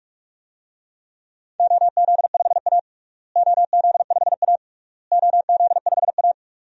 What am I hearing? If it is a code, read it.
Morse code, heard as a keyed tone, about 46 wpm, 700 Hz. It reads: OZ5A OZ5A OZ5A
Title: DXCC-100 Call Signs - 4 Characters - Repeated 3x - 45wpm